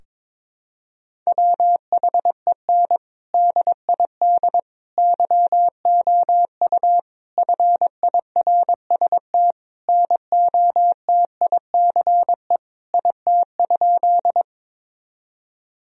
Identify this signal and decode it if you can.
Morse code, heard as a keyed tone, about 21 wpm, 700 Hz. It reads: WHEN DID YOU FIRST NOTICE IT?
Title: Sentences from Top 1000 Words - Review Part 2 - 22wpm